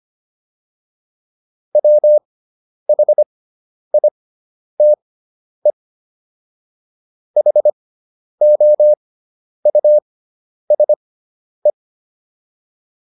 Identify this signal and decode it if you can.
Morse code, heard as a keyed tone, about 25 wpm, spaced as for 10 wpm, 600 Hz. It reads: WHITE HOUSE